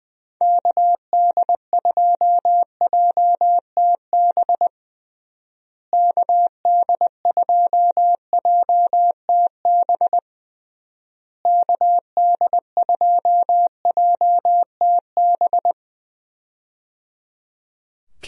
Morse code, heard as a keyed tone, about 19 wpm, 700 Hz. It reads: KD2JTB KD2JTB KD2JTB
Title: USA Call Signs - 6 Characters - Repeated 3x - 20wpm